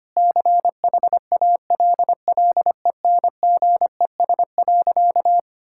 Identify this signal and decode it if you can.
Morse code, heard as a keyed tone, about 25 wpm, 700 Hz. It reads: CHALLENGES.
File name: practice / 25wpm / Sentences with 4-Word Phrases - 25wpm